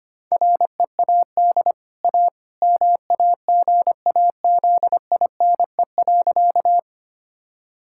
Morse code, heard as a keyed tone, about 25 wpm, 700 Hz. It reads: READ A MAGAZINE.
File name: Easy Sentences with Hints - 25wpm